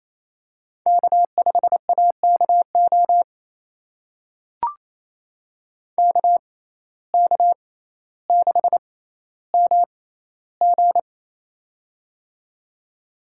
Morse code, heard as a keyed tone, about 28 wpm, spaced as for 14 wpm, 700 Hz. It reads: K5AKO KK6MG
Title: USA Call Signs - 5 Characters - 6x Character Spacing - 28wpm